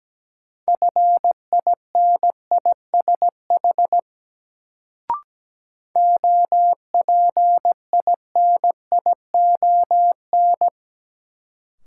Morse code, heard as a keyed tone, about 17 wpm, 700 Hz. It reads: FINISH OPINION
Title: Sentences from Top 1200 Words - New Word Drill Alt - 17wpm